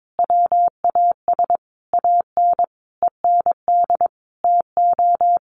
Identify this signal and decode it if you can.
Morse code, heard as a keyed tone, about 22 wpm, 700 Hz. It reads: WAS AN END TO
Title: Sentences from Top 1400 Words 22wpm